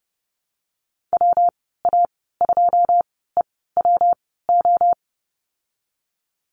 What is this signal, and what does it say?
Morse code, heard as a keyed tone, about 30 wpm, spaced as for 17 wpm, 700 Hz. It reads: WA2EWO
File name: Lesson 19g - Letter w - Call Signs 30wpm